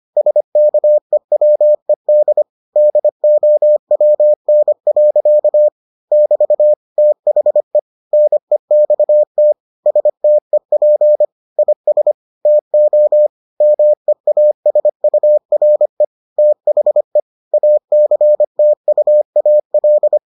Morse code, heard as a keyed tone, about 25 wpm, 600 Hz. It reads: SKEWED DOWN. <BT> THE NEXT STEP IS TO MEASURE THE ACTUAL